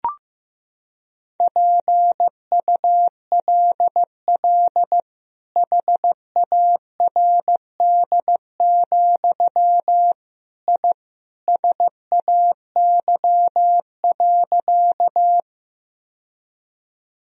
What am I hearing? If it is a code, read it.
Morse code, heard as a keyed tone, about 16 wpm, 700 Hz. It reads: PULL HARD, I SAY.
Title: Sentences from Top 1000 Words - Review Part 2 - 15wpm